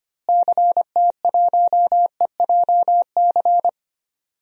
Morse code, heard as a keyed tone, about 25 wpm, 700 Hz. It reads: CT1EJC
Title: DXCC-100 Call Signs - 6 Characters - Repeated 2x - 25wpm